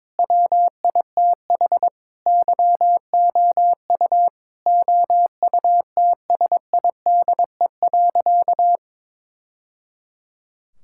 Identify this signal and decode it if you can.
Morse code, heard as a keyed tone, about 22 wpm, 700 Hz. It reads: WITH YOU OUTSIDE.
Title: Sentences from Top 1000 Words - Review Part 6 - 22wpm